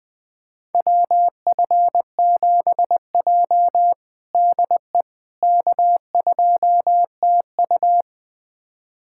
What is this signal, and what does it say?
Morse code, heard as a keyed tone, about 20 wpm, 700 Hz. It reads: WF7J DE K2TU
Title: USA Callsign DE Callsign - 4 Characters - 20wpm